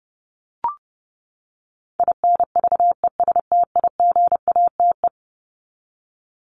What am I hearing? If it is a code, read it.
Morse code, heard as a keyed tone, about 30 wpm, 700 Hz. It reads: INVESTIGATE
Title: Top 1751-2000 Words 30wpm